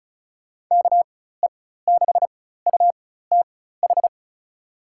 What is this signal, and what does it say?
Morse code, heard as a keyed tone, about 36 wpm, spaced as for 16 wpm, 700 Hz. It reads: KE6UTH
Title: USA Call Signs - 6 Characters - 4x Character Spacing - 35wpm